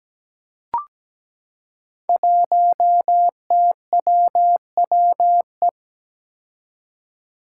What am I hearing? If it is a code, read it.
Morse code, heard as a keyed tone, about 17 wpm, 700 Hz. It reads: 1TWWE
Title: Five Challenging Letter-Number Combinations 17wpm